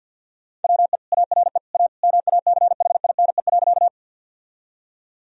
Morse code, heard as a keyed tone, about 50 wpm, 700 Hz. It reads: WE ARE A MACHINE.
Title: Sentences from Top 1000 Words - Review Part 2 - 50wpm